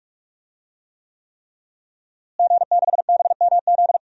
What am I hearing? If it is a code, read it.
Morse code, heard as a keyed tone, about 43 wpm, 700 Hz. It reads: G6BMZ
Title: DXCC-100 Call Signs - 5 Characters - Repeated 3x - 45wpm